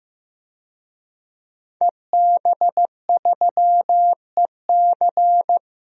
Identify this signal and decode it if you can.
Morse code, heard as a keyed tone, about 15 wpm, 700 Hz. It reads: EB3EC
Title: DXCC-100 Call Signs - 5 Characters - Repeated 2x - 15wpm